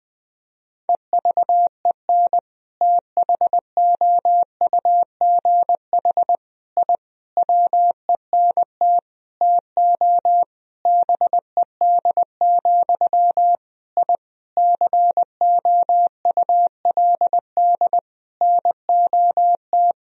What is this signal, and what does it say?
Morse code, heard as a keyed tone, about 20 wpm, 700 Hz. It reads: EVEN THOUGH I WENT TO BED, I COULD NOT
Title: Sentences from Top 1000 Words - Review Part 2 - 20wpm